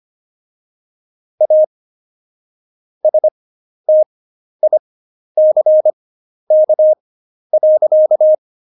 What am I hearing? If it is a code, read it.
Morse code, heard as a keyed tone, about 25 wpm, spaced as for 11 wpm, 620 Hz. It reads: A STICK.
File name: Jokes-25-char-10-eff